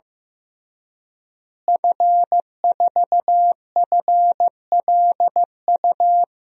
Morse code, heard as a keyed tone, about 16 wpm, 700 Hz. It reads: F4FLU
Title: DXCC-100 Call Signs - 5 Characters - Repeated 3x - 15wpm